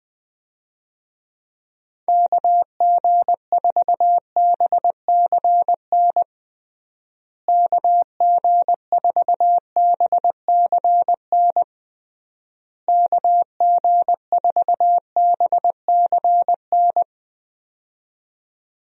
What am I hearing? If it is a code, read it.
Morse code, heard as a keyed tone, about 20 wpm, 700 Hz. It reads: KG4BCN KG4BCN KG4BCN
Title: USA Call Signs - 6 Characters - Repeated 3x - 20wpm